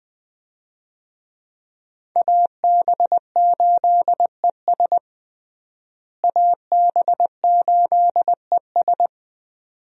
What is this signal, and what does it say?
Morse code, heard as a keyed tone, about 20 wpm, 700 Hz. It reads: AB8ES AB8ES
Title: USA Call Signs - 5 Characters - Repeated 2x - 20wpm